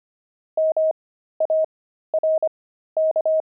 Morse code, heard as a keyed tone, about 25 wpm, spaced as for 13 wpm, 630 Hz. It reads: MARK
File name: pangram-fs-13-cs-25-javelins